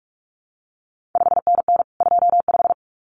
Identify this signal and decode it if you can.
Morse code, heard as a keyed tone, about 46 wpm, 700 Hz. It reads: 5NN 25
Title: CQ WW DX Contest Exchange Encore - 45wpm